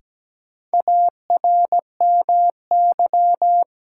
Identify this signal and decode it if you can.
Morse code, heard as a keyed tone, about 17 wpm, 700 Hz. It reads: ARMY
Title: Sentences from Top 800 Words - New Word Drill - 17wpm